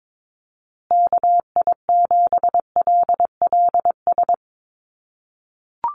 Morse code, heard as a keyed tone, about 22 wpm, 700 Hz. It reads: KI7LLS
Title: USA Call Signs Repeated 3x 22wpm